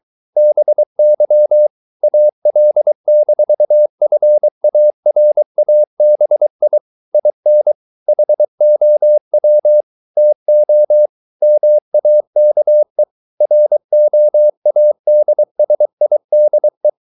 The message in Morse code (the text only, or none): BY AL-FARABI IN HOW TO MAKE ROADSIDE